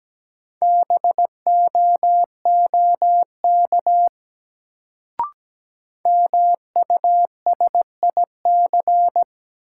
Morse code, heard as a keyed tone, about 17 wpm, 700 Hz. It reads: BOOK MUSIC
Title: Sentences from Top 300 Words - New Word Drill - 17wpm